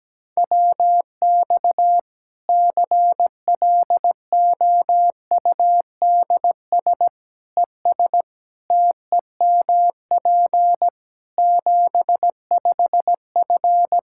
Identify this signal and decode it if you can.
Morse code, heard as a keyed tone, about 17 wpm, 700 Hz. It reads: WX CLOUDS ES TEMP 75F
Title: K1USN SST QSO Practice 17wpm